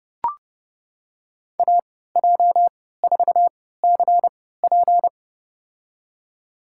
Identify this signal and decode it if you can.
Morse code, heard as a keyed tone, about 30 wpm, spaced as for 17 wpm, 700 Hz. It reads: AJ4CP
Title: USA Call Signs - 3x Character Spacing - 30wpm